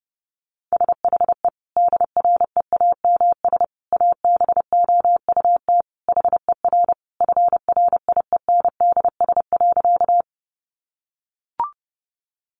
Morse code, heard as a keyed tone, about 30 wpm, 700 Hz. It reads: SHE DREAMS ABOUT HER FRIENDS.
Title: Easy Phrases 30wpm